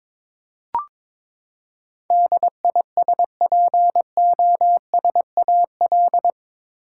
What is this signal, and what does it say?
Morse code, heard as a keyed tone, about 22 wpm, 700 Hz. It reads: DISPOSAL
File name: practice / 22wpm / Eight Letter Words - 22wpm